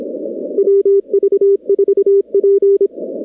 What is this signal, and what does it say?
Morse code, heard as a keyed tone, about 26 wpm, 410 Hz. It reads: WV4P
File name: contests / 2019-cw-wpx-kt5j-k5pi- -2330 0-41